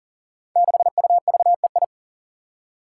Morse code, heard as a keyed tone, about 40 wpm, 700 Hz. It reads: 6UVEI
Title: Five Challenging Letter-Number Combinations 40wpm